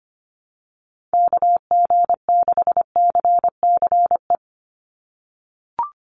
Morse code, heard as a keyed tone, about 25 wpm, 700 Hz. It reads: KG6CCE
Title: USA Call Signs 25wpm